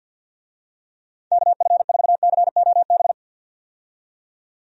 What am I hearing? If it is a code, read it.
Morse code, heard as a keyed tone, about 51 wpm, 700 Hz. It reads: KF4COB
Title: USA Call Signs - 6 Characters - 50wpm